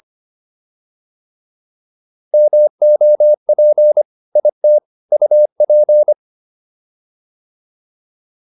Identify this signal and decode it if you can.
Morse code, heard as a keyed tone, about 25 wpm, 600 Hz. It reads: MOP IT UP